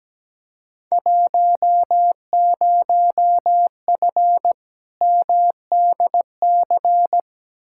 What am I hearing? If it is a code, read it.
Morse code, heard as a keyed tone, about 17 wpm, 700 Hz. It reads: 10F MDC
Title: Field Day Exchanges Encore - 17wpm